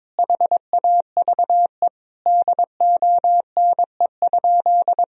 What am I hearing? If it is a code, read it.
Morse code, heard as a keyed tone, about 22 wpm, 700 Hz. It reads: HAVE DONE?
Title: Sentences from Top 1000 Words - Review Part 3 - 22wpm